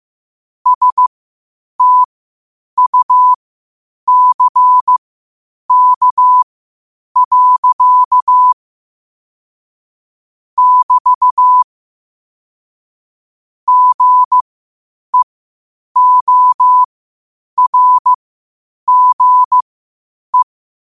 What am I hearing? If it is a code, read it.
Morse code, heard as a keyed tone, about 15 wpm, spaced as for 8 wpm, 1000 Hz. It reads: STUCK. <BT> GEORGE
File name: qotdfiles / 8wpm